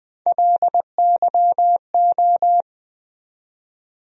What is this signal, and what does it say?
Morse code, heard as a keyed tone, about 21 wpm, 700 Hz. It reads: LYO
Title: USA Call Sign Postfixes - 3 Letters 20wpm